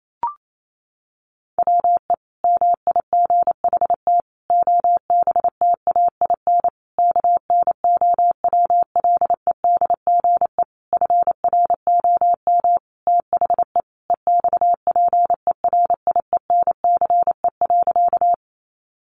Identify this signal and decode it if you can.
Morse code, heard as a keyed tone, about 28 wpm, 700 Hz. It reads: WE MIGHT OBTAIN KNOWLEDGE FROM THE EXPERIENCE.
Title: Sentences from Top 1300 Words 28wpm